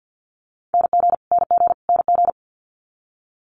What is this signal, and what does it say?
Morse code, heard as a keyed tone, about 50 wpm, 700 Hz. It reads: NG NG NG
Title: Multi-letter Phonemes Repeated 3x 50wpm